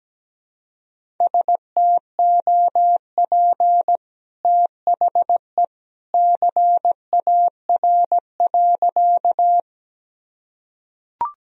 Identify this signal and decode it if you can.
Morse code, heard as a keyed tone, about 17 wpm, 700 Hz. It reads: STOP THE CAR.
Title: Sentences from Top 1000 Words - Review Part 1 - 17wpm